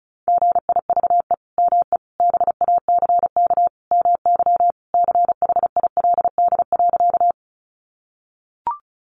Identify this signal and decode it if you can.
Morse code, heard as a keyed tone, about 34 wpm, 700 Hz. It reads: GIVE ME BACK MY CHILD.
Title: Sentences from Top 800 Words 35wpm